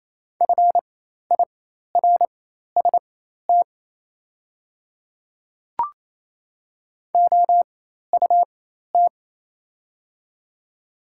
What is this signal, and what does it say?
Morse code, heard as a keyed tone, about 28 wpm, spaced as for 13 wpm, 700 Hz. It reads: FIRST OUT
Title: 4x Character Spacing - Top 100 Words - 28wpm